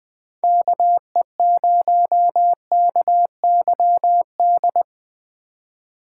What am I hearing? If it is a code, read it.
Morse code, heard as a keyed tone, about 20 wpm, 700 Hz. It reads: KE0KYD
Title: Speed Racing - USA 2x3 Call Signs 17wpm